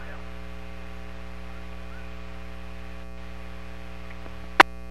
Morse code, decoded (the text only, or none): none